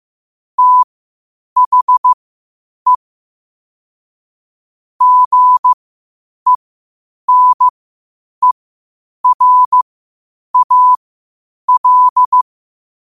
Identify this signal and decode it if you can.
Morse code, heard as a keyed tone, about 15 wpm, spaced as for 8 wpm, 1000 Hz. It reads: THE GENERAL
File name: qotdfiles / 8wpm